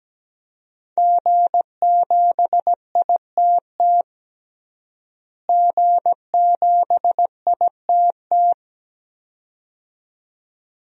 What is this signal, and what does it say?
Morse code, heard as a keyed tone, about 17 wpm, 700 Hz. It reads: G7ITT G7ITT
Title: DXCC-100 Call Signs - 5 Characters - Repeated 2x - 17wpm